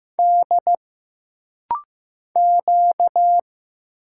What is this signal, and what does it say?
Morse code, heard as a keyed tone, about 15 wpm, 700 Hz. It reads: D Q
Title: Single Letter-Number Alt - Rapid-Fire 15wpm